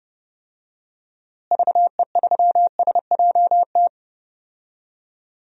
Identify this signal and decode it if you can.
Morse code, heard as a keyed tone, about 32 wpm, 700 Hz. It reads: VE3SJT
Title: DXCC-100 Call Signs - 6 Characters - Repeated 3x - 30wpm